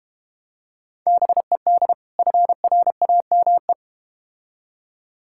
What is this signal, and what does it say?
Morse code, heard as a keyed tone, about 31 wpm, 700 Hz. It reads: BED FRAME
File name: Easy Words - Speed-Racing with Hints - 25wpm